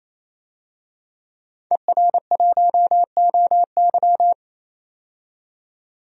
Morse code, heard as a keyed tone, about 28 wpm, 700 Hz. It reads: ER1OY
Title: DXCC-100 Call Signs - 5 Characters - 28wpm